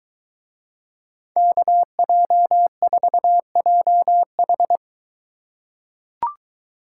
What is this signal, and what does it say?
Morse code, heard as a keyed tone, about 22 wpm, 700 Hz. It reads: KJ4JH